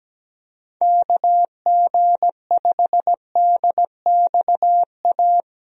Morse code, heard as a keyed tone, about 17 wpm, 700 Hz. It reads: KG5DXA
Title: USA Call Signs - 6 Characters - Repeated 2x - 17wpm